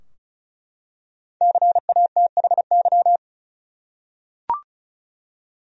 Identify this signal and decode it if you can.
Morse code, heard as a keyed tone, about 35 wpm, 700 Hz. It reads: CATHY
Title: USA Common Names - Max 5 Letters - 35wpm